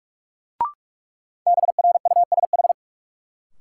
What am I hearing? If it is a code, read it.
Morse code, heard as a keyed tone, about 43 wpm, 700 Hz. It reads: BRUSH
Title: Sentences from Top 1100 Words - New Word Drill - 45wpm